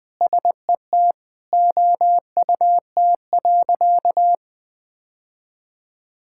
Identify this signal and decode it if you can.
Morse code, heard as a keyed tone, about 20 wpm, 700 Hz. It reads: SET OUT.
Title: Sentences from Top 1600 Words 20wpm